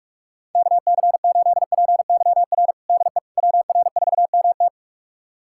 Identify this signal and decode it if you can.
Morse code, heard as a keyed tone, about 44 wpm, 690 Hz. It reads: KC8PYR DE WR4MT
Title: USA Callsign DE Callsign 45wpm